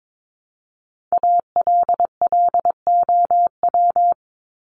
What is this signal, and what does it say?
Morse code, heard as a keyed tone, about 22 wpm, 700 Hz. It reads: ALLOW
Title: Top 501-750 Words 22wpm